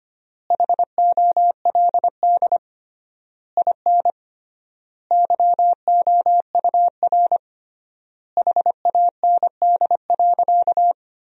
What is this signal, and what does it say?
Morse code, heard as a keyed tone, about 25 wpm, 700 Hz. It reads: HOLD IN YOUR HAND.
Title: 3x Word-Spacing - Sentences from Top 400 Words - 25wpm